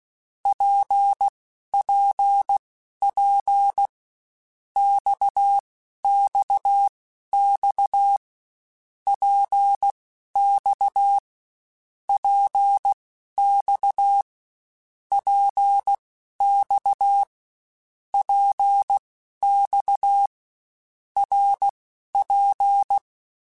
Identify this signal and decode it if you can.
Morse code, heard as a keyed tone, about 16 wpm, spaced as for 12 wpm, 800 Hz. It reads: PPP XXX PX PX PX PX RP
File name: Short Runs / Short Runs-mp3s / Balanced @15